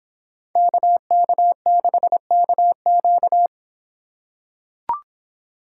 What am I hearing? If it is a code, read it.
Morse code, heard as a keyed tone, about 25 wpm, 700 Hz. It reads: KK6KQ